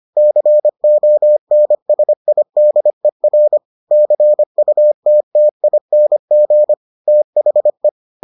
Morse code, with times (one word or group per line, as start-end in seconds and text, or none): 0.2-3.6 CONSIDER
3.9-6.7 CUTTING
7.1-7.9 THE